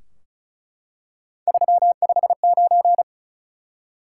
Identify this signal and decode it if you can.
Morse code, heard as a keyed tone, about 36 wpm, 700 Hz. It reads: 359
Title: Top 180 QSO Elements 35wpm